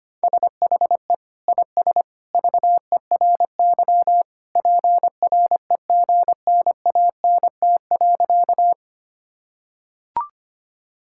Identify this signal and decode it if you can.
Morse code, heard as a keyed tone, about 25 wpm, 700 Hz. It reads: SHE IS VERY PREGNANT.
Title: Sentences from Top 1000 Words - Review Part 6 - 25wpm